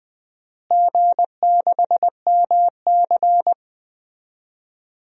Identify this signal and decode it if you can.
Morse code, heard as a keyed tone, about 20 wpm, 700 Hz. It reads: G6MC
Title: DXCC-100 Call Signs - 4 Characters - Repeated 2x - 20wpm